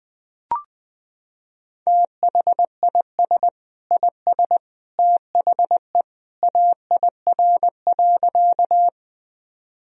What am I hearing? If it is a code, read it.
Morse code, heard as a keyed tone, about 20 wpm, 700 Hz. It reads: THIS IS THE AIR.
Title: Sentences from Top 1000 Words - Review Part 1 - 20wpm